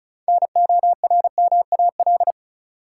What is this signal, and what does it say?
Morse code, heard as a keyed tone, about 36 wpm, 700 Hz. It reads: NORMAL
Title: Six Letter Words - 35wpm